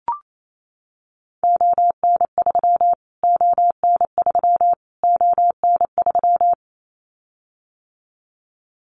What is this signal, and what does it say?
Morse code, heard as a keyed tone, about 28 wpm, 700 Hz. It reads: ON3 ON3 ON3
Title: Callsign Prefixes for Top 25 DXCC - Repeated 3x - 28wpm